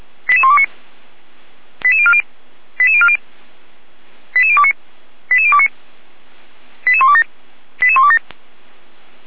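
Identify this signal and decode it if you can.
Five-tone selective call (ZVEI), heard as quick runs of five tones, tones of about 70 ms. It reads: 88129 88040 88040 88029 88029 88127 88127
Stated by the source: ZVEI five-tone selective call received with an SDR, from the Feuerwehr Orlamünde fire brigade audio server.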